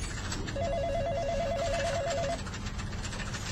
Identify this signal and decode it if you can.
There is no code.